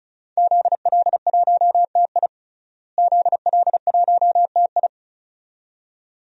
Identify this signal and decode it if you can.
Morse code, heard as a keyed tone, about 36 wpm, 700 Hz. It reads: ZL1TI ZL1TI
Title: DXCC-100 Call Signs - 5 Characters - Repeated 2x - 35wpm